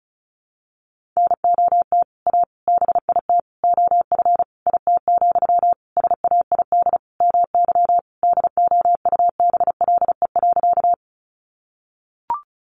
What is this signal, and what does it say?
Morse code, heard as a keyed tone, about 34 wpm, 700 Hz. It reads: NOT A BIT OF IT, SAID MY DOUBLE.
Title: Sentences from Top 1400 Words 35wpm